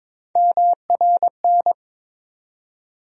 Morse code, heard as a keyed tone, about 22 wpm, 700 Hz. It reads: MRN
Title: USA Call Sign Postfixes - 3 Letters 22wpm